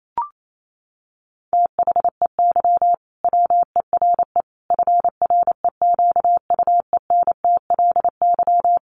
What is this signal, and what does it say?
Morse code, heard as a keyed tone, about 28 wpm, 700 Hz. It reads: THEY WERE FREQUENTLY
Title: Sentences from Top 1400 Words 28wpm